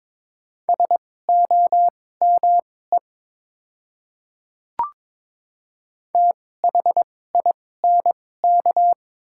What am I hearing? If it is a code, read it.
Morse code, heard as a keyed tone, about 22 wpm, spaced as for 16 wpm, 700 Hz. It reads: SOME THINK
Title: 2x Character Spacing - Top 200 Words - 22wpm